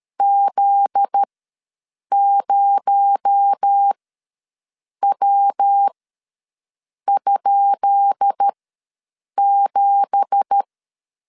Morse code, heard as a keyed tone, about 13 wpm, spaced as for 6 wpm, 800 Hz. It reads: Z0W?7
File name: LEZIONI / 10 SIGNS / 10B INT ws 12wpm 3527